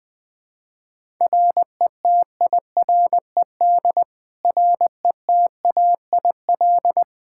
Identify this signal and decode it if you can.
Morse code, heard as a keyed tone, about 20 wpm, 700 Hz. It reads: RETIRED RETAIL
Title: K1USN SST QSO Practice 20wpm